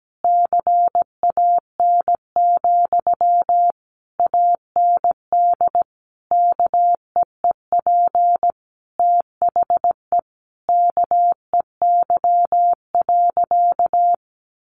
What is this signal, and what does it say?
Morse code, heard as a keyed tone, about 17 wpm, 700 Hz. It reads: CAN, AND KEEP THE KEY.
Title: Sentences from Top 1300 Words 17wpm